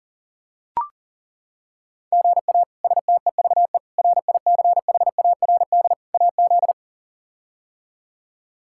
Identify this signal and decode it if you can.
Morse code, heard as a keyed tone, about 40 wpm, 700 Hz. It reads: GA STEVE RICHARD AZ
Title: K1USN SST QSO Practice - 40wpm